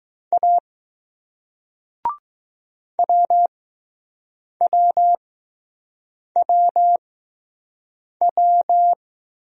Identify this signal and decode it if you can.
Morse code, heard as a keyed tone, about 20 wpm, spaced as for 6 wpm, 700 Hz. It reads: A WWWW